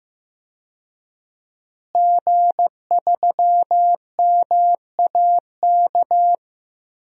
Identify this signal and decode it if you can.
Morse code, heard as a keyed tone, about 15 wpm, 700 Hz. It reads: G3MAK
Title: DXCC-100 Call Signs - 5 Characters - Repeated 2x - 15wpm